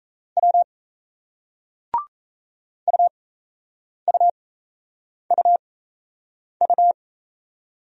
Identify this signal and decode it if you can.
Morse code, heard as a keyed tone, about 38 wpm, spaced as for 8 wpm, 700 Hz. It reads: W UUUU